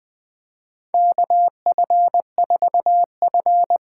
Morse code, heard as a keyed tone, about 20 wpm, 700 Hz. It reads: KF4F
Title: USA Call Signs - 8x Character Spacing - 20wpm